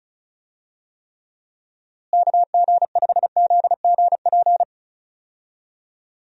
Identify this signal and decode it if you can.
Morse code, heard as a keyed tone, about 36 wpm, 700 Hz. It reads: KG5ZGP
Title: USA Call Signs - 6 Characters - Repeated 3x - 35wpm